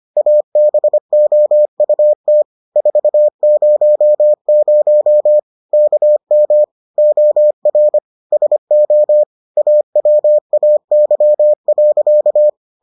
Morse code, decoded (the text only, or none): ABOUT 400 KM OR SO AWAY.